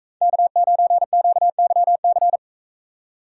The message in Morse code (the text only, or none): K9QYC